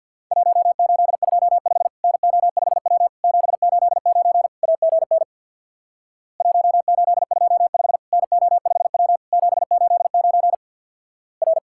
Morse code, heard as a keyed tone, about 50 wpm, 700 Hz. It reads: 1825 NO5W 789 AGN 1825 NO5W 789 R
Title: Worked All Europe DX Contest - QTC Exchange with Repeats - 50wpm